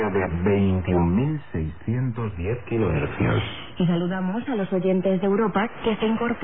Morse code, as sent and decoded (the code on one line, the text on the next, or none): none